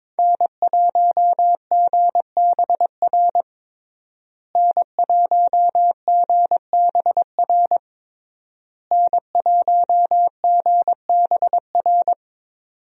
Morse code, heard as a keyed tone, about 22 wpm, 700 Hz. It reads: N1GBR N1GBR N1GBR
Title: USA Call Signs - 5 Characters - Repeated 3x - 22wpm